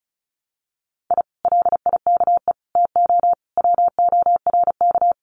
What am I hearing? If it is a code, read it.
Morse code, heard as a keyed tone, about 36 wpm, 700 Hz. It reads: I LIKE TO WORK
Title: Common QSO Phrases Encore 35wpm